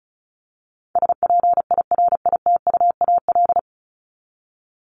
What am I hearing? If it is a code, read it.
Morse code, heard as a keyed tone, about 34 wpm, 700 Hz. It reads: SPIRITUAL
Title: Top 1501-1750 Words 35wpm